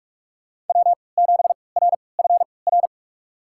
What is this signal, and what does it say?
Morse code, heard as a keyed tone, about 42 wpm, spaced as for 25 wpm, 700 Hz. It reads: W7RFR
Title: USA Call Signs - 5 Characters - 3x Character Spacing - 45wpm